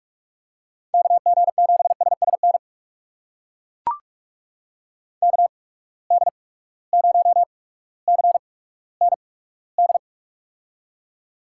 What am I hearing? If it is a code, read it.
Morse code, heard as a keyed tone, about 42 wpm, spaced as for 19 wpm, 700 Hz. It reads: KG7SSN KD0CND
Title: USA Call Signs - 6 Characters - 8x Character Spacing - 45wpm